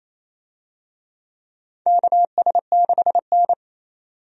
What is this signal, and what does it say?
Morse code, heard as a keyed tone, about 28 wpm, 700 Hz. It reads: KS6N
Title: USA Call Signs - 4 Characters - 28wpm